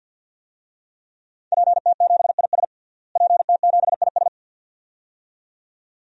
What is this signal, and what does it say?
Morse code, heard as a keyed tone, about 49 wpm, 700 Hz. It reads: PT7IS PT7IS
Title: DXCC-100 Call Signs - 5 Characters - Repeated 2x - 50wpm